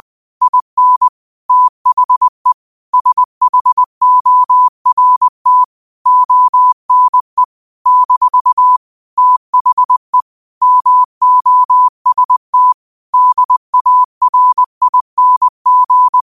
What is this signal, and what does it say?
Morse code, heard as a keyed tone, about 20 wpm, 1000 Hz. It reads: IN THE SHORT ONE - THE MOST DARING
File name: qotdfiles / 20wpm